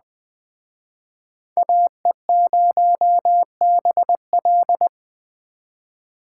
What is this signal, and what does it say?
Morse code, heard as a keyed tone, about 20 wpm, 700 Hz. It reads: AE0BL
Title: Speed Racing - USA 2x2 Call Signs 17wpm